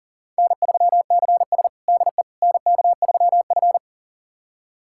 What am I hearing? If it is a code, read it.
Morse code, heard as a keyed tone, about 40 wpm, 700 Hz. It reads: N3CS DE NK3F
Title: USA Callsign DE Callsign - 4 Characters - 40wpm